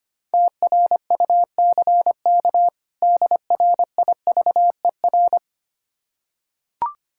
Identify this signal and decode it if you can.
Morse code, heard as a keyed tone, about 25 wpm, 700 Hz. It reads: TRUCK DRIVER
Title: Practice QSO - Exchange 3 - More Info - 25wpm